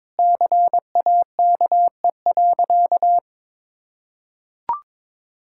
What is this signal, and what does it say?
Morse code, heard as a keyed tone, about 22 wpm, 700 Hz. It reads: CAKE.
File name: practice / 22wpm / Sentences with 2-Word Phrases - 22wpm